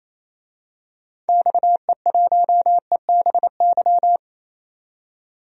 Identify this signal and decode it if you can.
Morse code, heard as a keyed tone, about 28 wpm, 700 Hz. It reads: XE1EBY
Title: DXCC-100 Call Signs - 6 Characters - 28wpm